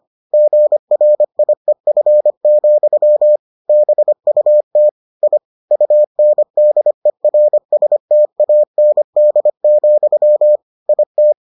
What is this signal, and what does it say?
Morse code, heard as a keyed tone, about 25 wpm, 600 Hz. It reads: GRIEF, BUT I UNDERSTAND, IT